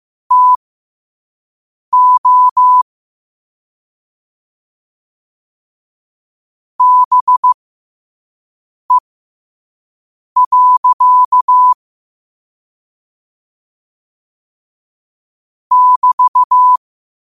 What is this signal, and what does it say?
Morse code, heard as a keyed tone, about 15 wpm, spaced as for 5 wpm, 1000 Hz. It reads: TO BE. <BT>
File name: qotdfiles / 5wpm